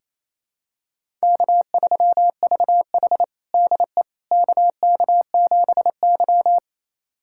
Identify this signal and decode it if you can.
Morse code, heard as a keyed tone, about 27 wpm, 700 Hz. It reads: K3VH DE KK7Y
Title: USA Callsign DE Callsign - 4 Characters - 28wpm